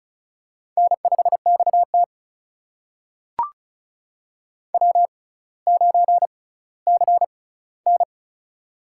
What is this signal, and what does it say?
Morse code, heard as a keyed tone, about 36 wpm, spaced as for 18 wpm, 700 Hz. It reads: N5XT W9CN
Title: USA Call Signs - 4 Characters - 6x Character Spacing - 35wpm